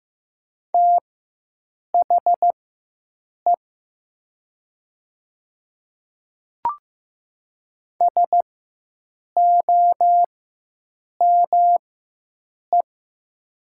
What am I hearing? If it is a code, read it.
Morse code, heard as a keyed tone, about 15 wpm, spaced as for 7 wpm, 700 Hz. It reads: THE SOME
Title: 4x Character Spacing - Top 200 Words - 15wpm